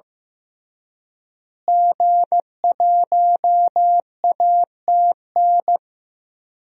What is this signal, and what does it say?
Morse code, heard as a keyed tone, about 16 wpm, 700 Hz. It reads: G1ATN
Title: DXCC-100 Call Signs - 5 Characters - Repeated 3x - 15wpm